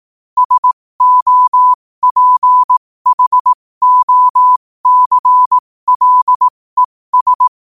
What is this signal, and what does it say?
Morse code, heard as a keyed tone, about 18 wpm, spaced as for 15 wpm, 1000 Hz. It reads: SOPHOCLES
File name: qotdfiles / 15wpm